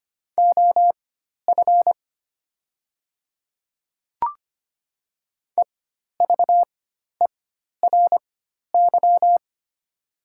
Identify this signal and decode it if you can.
Morse code, heard as a keyed tone, about 25 wpm, spaced as for 12 wpm, 700 Hz. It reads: OF EVERY
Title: 4x Character Spacing - Top 500 Words - 25wpm